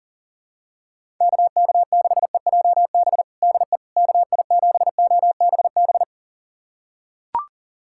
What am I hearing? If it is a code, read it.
Morse code, heard as a keyed tone, about 40 wpm, 700 Hz. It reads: KK6EJB DE KI7OBB
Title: USA Callsign DE Callsign - 6 Characters - 40wpm